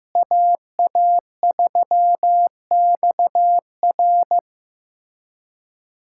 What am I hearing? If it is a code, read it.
Morse code, heard as a keyed tone, about 16 wpm, 700 Hz. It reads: AA3XR
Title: USA Callsign DE Callsign - 5 Characters - 15wpm